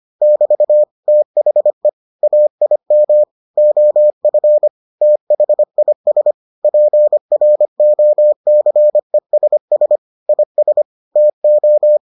Morse code, heard as a keyed tone, about 25 wpm, 600 Hz. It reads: <BT> THE AIM OF THIS PROCESS IS TO